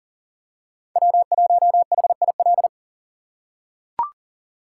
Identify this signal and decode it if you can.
Morse code, heard as a keyed tone, about 40 wpm, 700 Hz. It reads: W1HIL
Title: USA Call Signs - 5 Characters - 4x Character Spacing - 40wpm